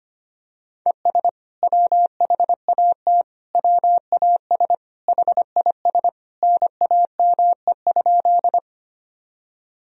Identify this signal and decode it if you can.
Morse code, heard as a keyed tone, about 25 wpm, 700 Hz. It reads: ES WHAT WAS HIS NAME?
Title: Sentences from Top 200 Words and Abbreviations 25wpm